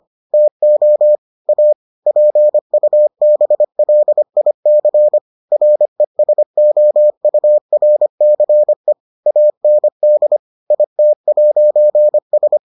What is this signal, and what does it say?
Morse code, heard as a keyed tone, about 25 wpm, 600 Hz. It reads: TO A PUBLIC RESOURCE AND IT'S